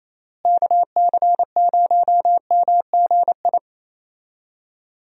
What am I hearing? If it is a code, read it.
Morse code, heard as a keyed tone, about 28 wpm, 700 Hz. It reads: KC0MGI
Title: USA Call Signs - 6 Characters - Repeated 3x - 28wpm